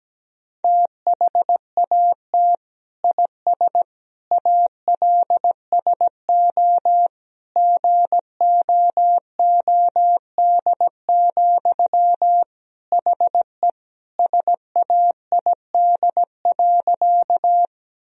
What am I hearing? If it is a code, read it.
Morse code, heard as a keyed tone, about 17 wpm, 700 Hz. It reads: THAT IS ALSO GOOD, HE SAID.
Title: Sentences from Top 1000 Words - Review Part 1 - 17wpm